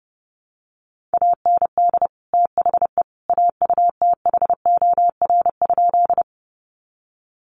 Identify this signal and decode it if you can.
Morse code, heard as a keyed tone, about 30 wpm, 700 Hz. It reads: AND THE AUTHOR?
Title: Sentences from Top 1200 Words 30wpm